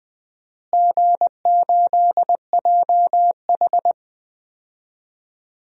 Morse code, heard as a keyed tone, about 20 wpm, 700 Hz. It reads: G8JH
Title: DXCC-100 Call Signs - 4 Characters - Repeated 3x - 20wpm